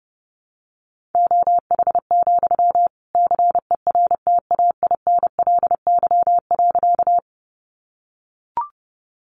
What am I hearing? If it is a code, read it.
Morse code, heard as a keyed tone, about 30 wpm, 700 Hz. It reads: OH, CERTAINLY.
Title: Sentences from Top 900 Words 30wpm